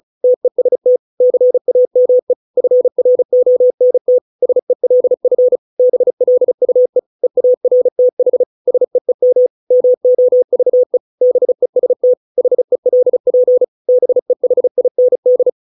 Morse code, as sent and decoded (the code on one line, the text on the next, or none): - . ... - / -.-. .- -- . / ..-. .-. --- -. - / ... .-.. ..-. / -... .-.. ..- . / . .- .-. - .... / ... . -- / -- --- ...- . / -... ... - / .... .-.. .--. / -... .... .. -. -..
TEST CAME FRONT SELF BLUE EARTH SEEM MOVE BEST HELP BEHIND